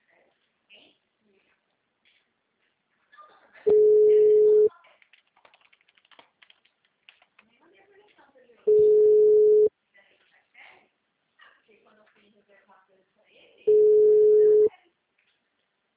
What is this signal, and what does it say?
Ringback tone (Europe, 425 Hz), with speech.